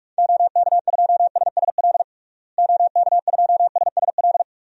Morse code, heard as a keyed tone, about 46 wpm, 700 Hz. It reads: OK2SSL OK2SSL
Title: DXCC-100 Call Signs - 6 Characters - Repeated 2x - 45wpm